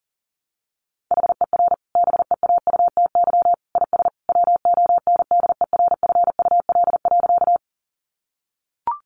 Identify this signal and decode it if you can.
Morse code, heard as a keyed tone, about 40 wpm, 700 Hz. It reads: HER BEAUTY IS WONDERFUL.
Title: Sentences from Top 1100 Words 40wpm